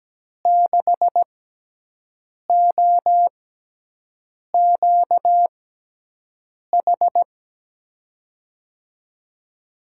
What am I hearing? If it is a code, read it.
Morse code, heard as a keyed tone, about 17 wpm, spaced as for 6 wpm, 700 Hz. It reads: 6OQH
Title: USA Call Signs - 6 Characters - 6x Character Spacing - 17wpm